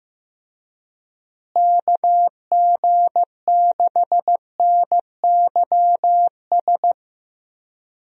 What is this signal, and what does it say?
Morse code, heard as a keyed tone, about 15 wpm, 700 Hz. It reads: KG6NYS
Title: USA Call Signs - 6 Characters - Repeated 3x - 15wpm